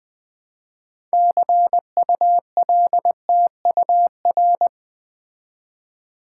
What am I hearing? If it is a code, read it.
Morse code, heard as a keyed tone, about 20 wpm, 700 Hz. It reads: CULTUR
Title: Build A Word - 2 Letters At A Time - 8-Letter Words - 20wpm